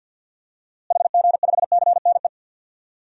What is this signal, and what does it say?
Morse code, heard as a keyed tone, about 53 wpm, 700 Hz. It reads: HG5CNE
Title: DXCC-100 Call Signs - 6 Characters - Repeated 2x - 50wpm